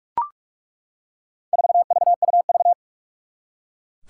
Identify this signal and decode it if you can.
Morse code, heard as a keyed tone, about 41 wpm, 700 Hz. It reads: 4VUV
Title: Four Challenging Letter-Number Combinations 45wpm